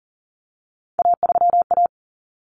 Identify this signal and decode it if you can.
Morse code, heard as a keyed tone, about 40 wpm, 700 Hz. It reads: A3A
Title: Common Cut Numbers - 3-Digit 40wpm